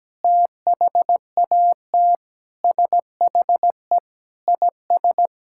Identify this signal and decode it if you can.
Morse code, heard as a keyed tone, about 17 wpm, 700 Hz. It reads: THAT SHE IS
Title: Sentences from Top 1000 Words - Review Part 6 - 17wpm